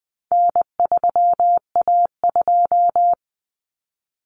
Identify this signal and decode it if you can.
Morse code, heard as a keyed tone, about 20 wpm, 700 Hz. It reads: N3A2
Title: Common Cut Numbers - 4-Digit 20wpm